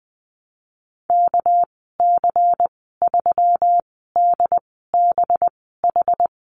Morse code, heard as a keyed tone, about 20 wpm, spaced as for 15 wpm, 700 Hz. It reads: KC3DBH
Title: 207 Lesson 33g - Letter j - Call Signs 20wpm